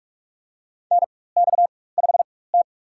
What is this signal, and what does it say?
Morse code, heard as a keyed tone, about 46 wpm, spaced as for 21 wpm, 700 Hz. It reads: NX5T
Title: USA Call Signs - 4x Character Spacing - 45wpm